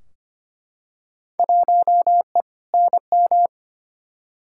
Morse code, heard as a keyed tone, about 25 wpm, 700 Hz. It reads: 1E NM
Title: Field Day Exchanges - 25wpm